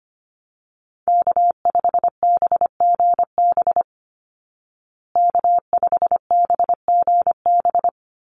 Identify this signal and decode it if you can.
Morse code, heard as a keyed tone, about 25 wpm, 700 Hz. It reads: K5BGB K5BGB
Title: Call Signs Repeated 3x - 25wpm